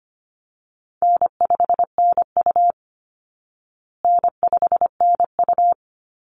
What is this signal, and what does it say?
Morse code, heard as a keyed tone, about 25 wpm, 700 Hz. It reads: N5NU N5NU
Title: Call Signs Repeated 3x - 25wpm